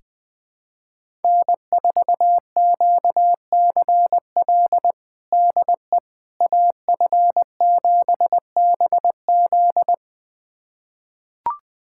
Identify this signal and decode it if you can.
Morse code, heard as a keyed tone, about 21 wpm, 700 Hz. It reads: N4QCL DE AF7BZ
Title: USA Callsign DE Callsign - 5 Characters - 20wpm